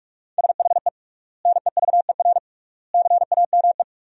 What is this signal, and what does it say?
Morse code, heard as a keyed tone, about 41 wpm, 700 Hz. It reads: SHE NEVER CAME
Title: 3x Word-Spacing - Sentences from Top 300 Words - 45wpm